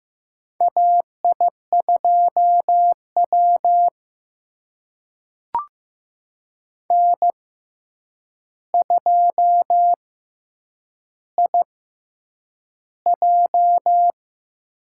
Morse code, heard as a keyed tone, about 15 wpm, spaced as for 8 wpm, 700 Hz. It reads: AI2W N2IJ
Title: USA Call Signs - 6x Character Spacing - 15wpm